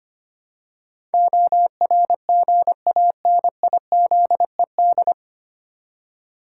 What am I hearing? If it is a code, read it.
Morse code, heard as a keyed tone, about 25 wpm, 700 Hz. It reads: ORGANIZED
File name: Nine Letter Words - 25wpm